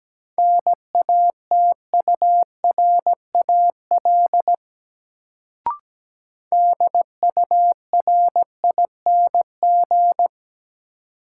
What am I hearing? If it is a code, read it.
Morse code, heard as a keyed tone, about 17 wpm, 700 Hz. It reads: NATURAL DURING
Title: Sentences from Top 700 Words - New Word Drill - 17wpm